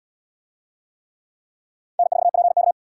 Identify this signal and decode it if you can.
Morse code, heard as a keyed tone, about 73 wpm, 700 Hz. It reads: N4PK